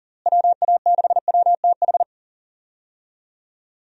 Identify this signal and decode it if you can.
Morse code, heard as a keyed tone, about 40 wpm, 700 Hz. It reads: WA6WTH
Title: USA Call Signs - 6 Characters - Repeated 3x - 40wpm